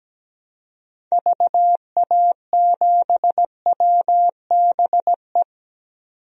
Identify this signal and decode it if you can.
Morse code, heard as a keyed tone, about 17 wpm, 700 Hz. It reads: VA7WBE
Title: DXCC-100 Call Signs - 6 Characters - 17wpm